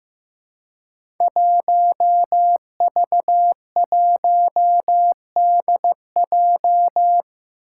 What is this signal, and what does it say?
Morse code, heard as a keyed tone, about 15 wpm, 700 Hz. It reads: 1V1DJ
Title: Five Challenging Letter-Number Combinations 15wpm